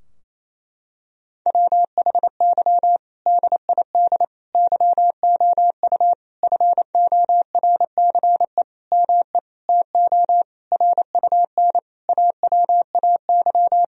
Morse code, heard as a keyed tone, about 27 wpm, 700 Hz. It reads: WHY DID YOU FORCE ME TO RUN AWAY
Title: Sentences from Top 1000 Words - Review Part 3 - 28wpm